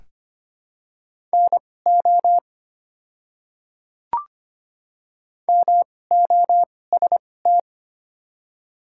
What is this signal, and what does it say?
Morse code, heard as a keyed tone, about 25 wpm, spaced as for 18 wpm, 700 Hz. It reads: NO MOST
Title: 2x Character Spacing - Top 100 Words - 25wpm